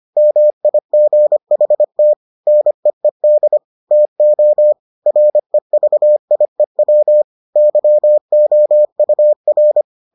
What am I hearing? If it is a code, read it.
Morse code, heard as a keyed tone, about 25 wpm, 600 Hz. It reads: MIGHT NEED TO REVIEW YOUR